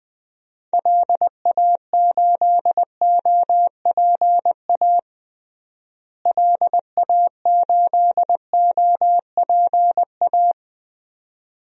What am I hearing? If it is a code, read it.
Morse code, heard as a keyed tone, about 20 wpm, 700 Hz. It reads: LA8OPA LA8OPA
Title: DXCC-100 Call Signs - 6 Characters - Repeated 2x - 20wpm